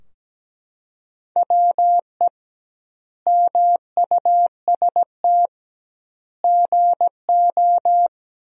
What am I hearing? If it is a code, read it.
Morse code, heard as a keyed tone, about 17 wpm, 700 Hz. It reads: WE MUST GO
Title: 2x Word-Spacing - Sentences from Top 300 Words - 17wpm